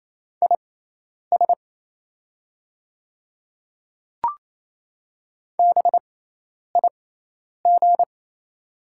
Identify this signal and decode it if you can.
Morse code, heard as a keyed tone, about 28 wpm, spaced as for 10 wpm, 700 Hz. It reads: IS BIG
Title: 6x Character Spacing - Top 200 Words - 28wpm